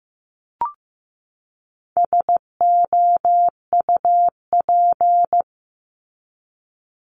Morse code, heard as a keyed tone, about 15 wpm, 700 Hz. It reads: SOUP
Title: Top 751-1000 Words 15wpm